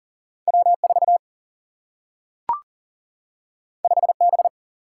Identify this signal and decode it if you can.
Morse code, heard as a keyed tone, about 40 wpm, 700 Hz. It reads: W4 5B
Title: Single-Letter Single-Number 40wpm